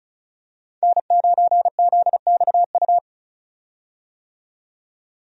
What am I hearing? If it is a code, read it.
Morse code, heard as a keyed tone, about 34 wpm, 700 Hz. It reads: N9ZXU
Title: USA Call Signs - 5 Characters - 35wpm